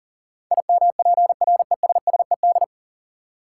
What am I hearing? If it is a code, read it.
Morse code, heard as a keyed tone, about 40 wpm, 700 Hz. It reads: IMPRESSED
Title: Sentences from Top 1600 Words - New Word Drill Alt - 40wpm